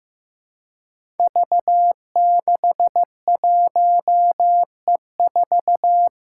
Morse code, heard as a keyed tone, about 15 wpm, 700 Hz. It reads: V61E4
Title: Five Challenging Letter-Number Combinations 15wpm